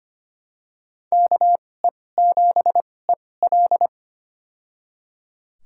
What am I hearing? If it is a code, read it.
Morse code, heard as a keyed tone, about 25 wpm, spaced as for 18 wpm, 700 Hz. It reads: KE7EL
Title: USA Call Signs - 5 Characters - 2x Character Spacing - 25wpm